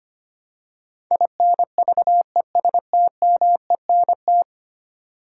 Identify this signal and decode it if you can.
Morse code, heard as a keyed tone, about 25 wpm, 700 Hz. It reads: INVESTMENT
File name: Ten Letter Words - 25wpm